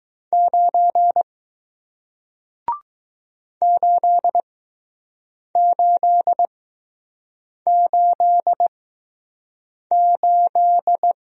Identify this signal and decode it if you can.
Morse code, heard as a keyed tone, about 20 wpm, spaced as for 6 wpm, 700 Hz. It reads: 9 8888